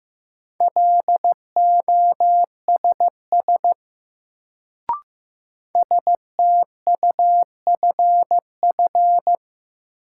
Morse code, heard as a keyed tone, about 15 wpm, 700 Hz. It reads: LOSS STUFF
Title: Sentences from Top 1400 Words - New Word Drill Alt - 15wpm